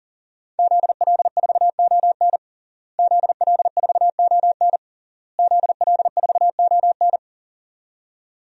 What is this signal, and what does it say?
Morse code, heard as a keyed tone, about 37 wpm, 700 Hz. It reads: ZL4ON ZL4ON ZL4ON
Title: DXCC-100 Call Signs - 5 Characters - Repeated 3x - 40wpm